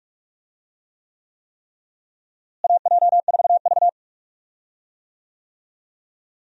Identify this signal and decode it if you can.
Morse code, heard as a keyed tone, about 42 wpm, 700 Hz. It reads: AJ4V
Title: USA Call Signs - 4 Characters - Repeated 3x - 45wpm